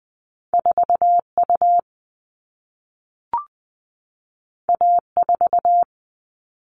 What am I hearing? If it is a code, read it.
Morse code, heard as a keyed tone, about 20 wpm, 700 Hz. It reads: 4 U A 4
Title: All Cut Numbers - 2-Digit 20wpm